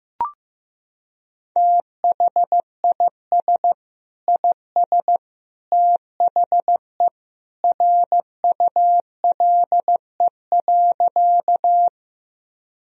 Morse code, heard as a keyed tone, about 16 wpm, 700 Hz. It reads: THIS IS THE RULE.
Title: Sentences from Top 1000 Words - Review Part 2 - 15wpm